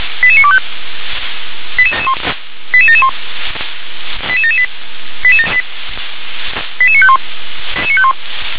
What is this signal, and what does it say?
Five-tone selective call (ZVEI), heard as quick runs of five tones, tones of about 70 ms. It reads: 88925 88771 88778 88951